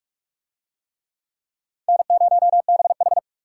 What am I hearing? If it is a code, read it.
Morse code, heard as a keyed tone, about 39 wpm, 700 Hz. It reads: N0BH